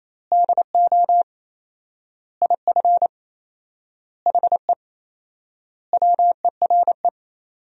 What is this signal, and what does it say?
Morse code, heard as a keyed tone, about 28 wpm, 700 Hz. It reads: DO IF HE WERE
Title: 4x Word-Spacing - Sentences from Top 400 Words - 28wpm